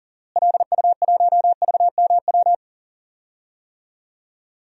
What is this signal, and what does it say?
Morse code, heard as a keyed tone, about 40 wpm, 700 Hz. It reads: LU1VMW
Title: DXCC-100 Call Signs - 6 Characters - Repeated 3x - 40wpm